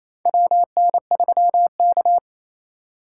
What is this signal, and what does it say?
Morse code, heard as a keyed tone, about 28 wpm, 700 Hz. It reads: WN3K
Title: K1USN SST QSO Practice - 28wpm